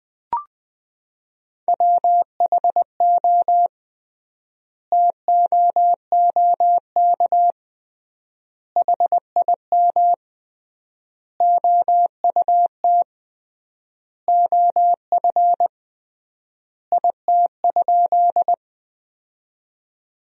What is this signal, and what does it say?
Morse code, heard as a keyed tone, about 20 wpm, 700 Hz. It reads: WHO TOOK HIM OUT OF IT?
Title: 3x Word-Spacing - Sentences from Top 300 Words - 20wpm